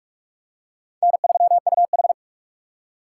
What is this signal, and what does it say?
Morse code, heard as a keyed tone, about 42 wpm, 700 Hz. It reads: N3UH